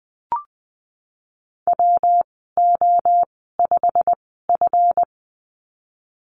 Morse code, heard as a keyed tone, about 20 wpm, spaced as for 15 wpm, 700 Hz. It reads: WO5F
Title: 167 Lesson 27g - Number 7 - Call Signs 20wpm